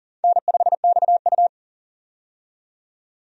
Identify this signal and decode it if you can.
Morse code, heard as a keyed tone, about 38 wpm, 700 Hz. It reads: N5XU
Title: USA Call Signs - 4 Characters - 40wpm